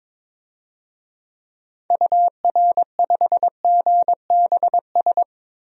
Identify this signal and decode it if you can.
Morse code, heard as a keyed tone, about 22 wpm, 700 Hz. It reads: UR5GBS
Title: DXCC-100 Call Signs - 6 Characters - Repeated 2x - 22wpm